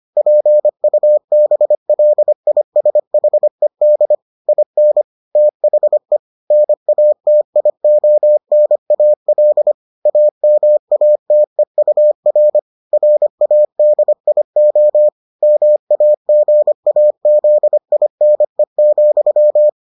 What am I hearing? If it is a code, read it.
Morse code, heard as a keyed tone, about 25 wpm, 600 Hz. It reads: PUBLISHED IN THE NATIONAL AMATEUR RADIO MAGAZINE,